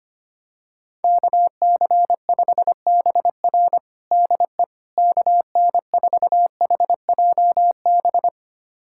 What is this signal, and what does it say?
Morse code, heard as a keyed tone, about 25 wpm, 700 Hz. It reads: KC5BR DE KN4HJB
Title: USA Callsign DE Callsign 25wpm